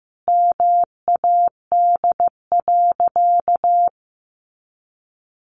Morse code, heard as a keyed tone, about 15 wpm, 700 Hz. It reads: MAD.
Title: Additional Easy Phrases 15wpm